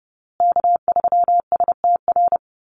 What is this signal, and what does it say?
Morse code, heard as a keyed tone, about 30 wpm, 700 Hz. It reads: K3STR
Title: USA Call Signs Repeated 2x 30wpm